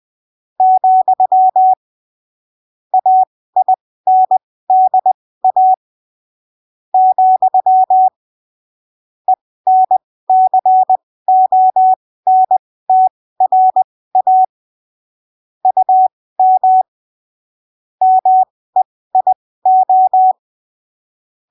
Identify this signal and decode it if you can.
Morse code, heard as a keyed tone, about 20 wpm, spaced as for 15 wpm, 750 Hz. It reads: , AINDA , ENCONTRA UM MEIO
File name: arquivos / texto / texto 23 20ppm